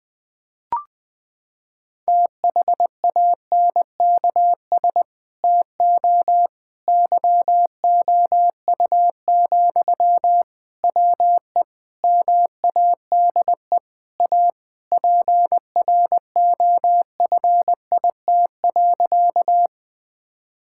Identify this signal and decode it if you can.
Morse code, heard as a keyed tone, about 21 wpm, 700 Hz. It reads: THANKS TO YOU, WE MADE A PROFIT.
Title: Sentences from Top 1000 Words - Review Part 5 - 20wpm